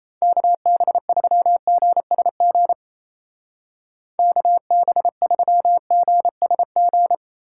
Morse code, heard as a keyed tone, about 34 wpm, 700 Hz. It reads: KB3GSG KB3GSG
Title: Speed Racing - USA 2x3 Call Signs 28wpm